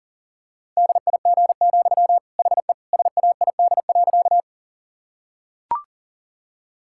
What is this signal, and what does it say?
Morse code, heard as a keyed tone, about 40 wpm, 700 Hz. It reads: DIG, HE SAID.
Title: Sentences from Top 1600 Words 40wpm